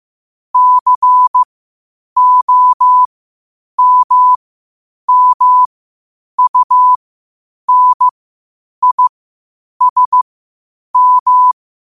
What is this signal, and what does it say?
Morse code, heard as a keyed tone, about 15 wpm, spaced as for 8 wpm, 1000 Hz. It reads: COMMUNISM
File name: qotdfiles / 8wpm